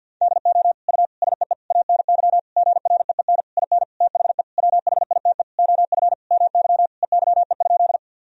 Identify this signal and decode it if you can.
Morse code, heard as a keyed tone, about 50 wpm, 700 Hz. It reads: DO U SEE ANY GREEN IN THE WHITE OF MY EYE?
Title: Sentences from Top 500 Words and Abbreviations 50wpm